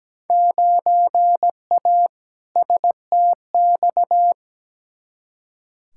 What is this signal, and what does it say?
Morse code, heard as a keyed tone, about 17 wpm, 700 Hz. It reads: 9A STX
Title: Field Day Exchanges - 17wpm